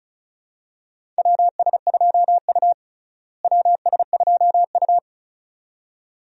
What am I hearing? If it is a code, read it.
Morse code, heard as a keyed tone, about 36 wpm, 700 Hz. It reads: WS2U WS2U
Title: USA Call Signs - 4 Characters - Repeated 2x - 35wpm